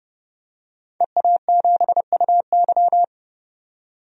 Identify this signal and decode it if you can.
Morse code, heard as a keyed tone, about 30 wpm, 700 Hz. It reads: EA7UY
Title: DXCC-100 Call Signs - 5 Characters - 30wpm